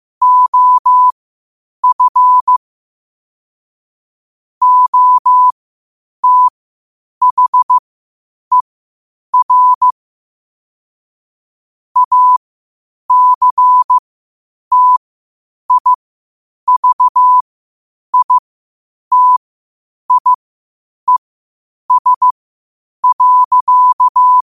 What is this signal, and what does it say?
Morse code, heard as a keyed tone, about 15 wpm, spaced as for 8 wpm, 1000 Hz. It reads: OF OTHER ACTIVITIES.
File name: qotdfiles / 8wpm